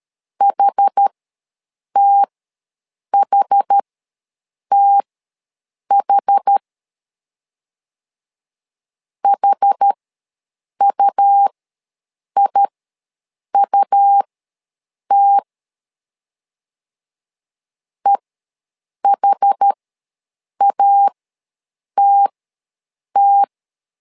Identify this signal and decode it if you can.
Morse code, heard as a keyed tone, about 13 wpm, spaced as for 7 wpm, 800 Hz. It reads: HTHTH HUIUT EHATT
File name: LEZIONI / 2 UHDN / 2 UH ws 12wpm 1730